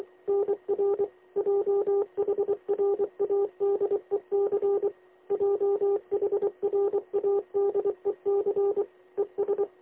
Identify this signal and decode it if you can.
Morse code, heard as a keyed tone, about 23 wpm, 400 Hz. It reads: NR JHRADEC JHRADEC ES
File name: lx1nj- -1005utc-10110khz2-49235